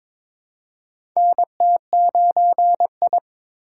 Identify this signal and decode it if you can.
Morse code, heard as a keyed tone, about 22 wpm, 700 Hz. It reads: NT9I
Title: USA Call Signs - 4 Characters - 3x Character Spacing - 22wpm